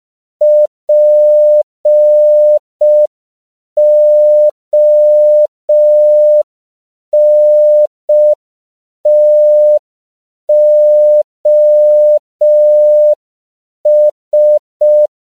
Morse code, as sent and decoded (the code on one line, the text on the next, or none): .--. --- -. - --- ...
PONTOS